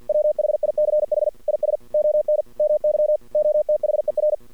Morse code, heard as a keyed tone, about 48 wpm, 600 Hz. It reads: WHICH IS ON MY OTHER
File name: archived / CRSnet